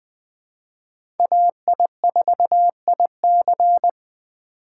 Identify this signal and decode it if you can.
Morse code, heard as a keyed tone, about 20 wpm, 700 Hz. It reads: AI4IC
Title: USA Call Signs - 5 Characters - 4x Character Spacing - 20wpm